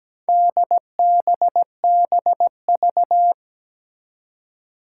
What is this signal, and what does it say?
Morse code, heard as a keyed tone, about 17 wpm, 700 Hz. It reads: DBBV
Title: Four Challenging Letter-Number Combinations 17wpm